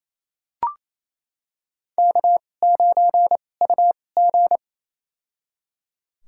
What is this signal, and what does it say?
Morse code, heard as a keyed tone, about 28 wpm, spaced as for 20 wpm, 700 Hz. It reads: K9UG
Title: USA Call Signs - 2x Character Spacing - 28wpm